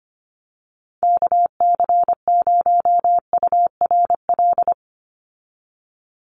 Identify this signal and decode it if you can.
Morse code, heard as a keyed tone, about 25 wpm, 700 Hz. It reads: KC0URL
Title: Call Signs - 25wpm